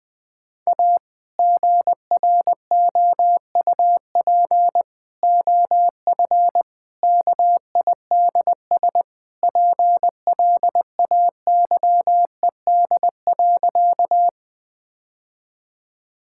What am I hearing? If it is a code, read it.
Morse code, heard as a keyed tone, about 20 wpm, 700 Hz. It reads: A GROUP OF KIDS PLAYED.
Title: Sentences with 3-Word Phrases - 20wpm